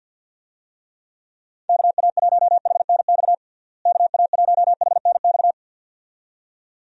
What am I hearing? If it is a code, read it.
Morse code, heard as a keyed tone, about 47 wpm, 700 Hz. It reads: KA1HNX KA1HNX
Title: USA Call Signs - 6 Characters - Repeated 2x - 50wpm